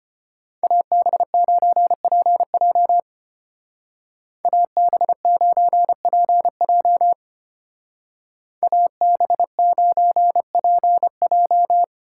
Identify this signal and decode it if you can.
Morse code, heard as a keyed tone, about 30 wpm, 700 Hz. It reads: AB9PJ AB9PJ AB9PJ
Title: Speed Racing - USA 2x2 Call Signs 25wpm